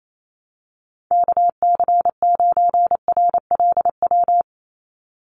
Morse code, heard as a keyed tone, about 28 wpm, 700 Hz. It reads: KC9RLW
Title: USA Call Signs 28wpm